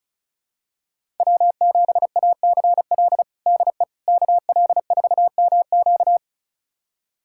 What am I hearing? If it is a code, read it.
Morse code, heard as a keyed tone, about 34 wpm, 700 Hz. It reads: W7ACL DE KL4MQ
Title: USA Callsign DE Callsign - 5 Characters - 35wpm